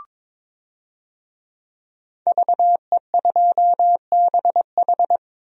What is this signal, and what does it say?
Morse code, heard as a keyed tone, about 22 wpm, 700 Hz. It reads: VE2BH
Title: DXCC-100 Call Signs - 5 Characters - 22wpm